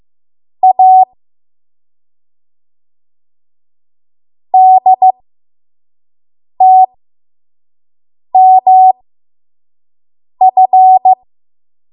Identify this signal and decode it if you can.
Morse code, heard as a keyed tone, about 15 wpm, spaced as for 5 wpm, 750 Hz. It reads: A DTMF
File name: Morse / 05WPM